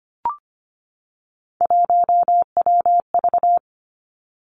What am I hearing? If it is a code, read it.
Morse code, heard as a keyed tone, about 25 wpm, 700 Hz. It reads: 1WV
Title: Three Challenging Letter-Number Combinations 25wpm